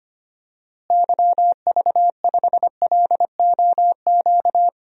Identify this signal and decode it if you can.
Morse code, heard as a keyed tone, about 25 wpm, 700 Hz. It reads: YV5LOQ
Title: DXCC-100 Call Signs - 6 Characters - Repeated 2x - 25wpm